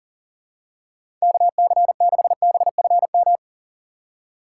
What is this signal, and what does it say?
Morse code, heard as a keyed tone, about 40 wpm, 700 Hz. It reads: KC6BFM
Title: USA Call Signs - 6 Characters - 3x Character Spacing - 40wpm